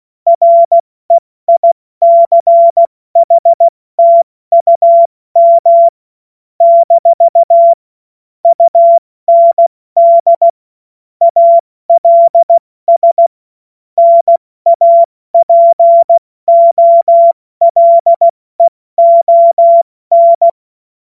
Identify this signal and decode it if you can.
Morse code, heard as a keyed tone, about 16 wpm, 680 Hz. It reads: REICHTUM - UND ALS NAPOLEON
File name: Morsekurs / Witze / Witz 19 März 2024 70 F80